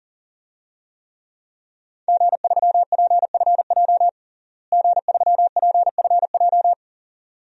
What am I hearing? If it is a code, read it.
Morse code, heard as a keyed tone, about 37 wpm, 700 Hz. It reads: G3PFJ G3PFJ
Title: DXCC-100 Call Signs - 5 Characters - Repeated 2x - 40wpm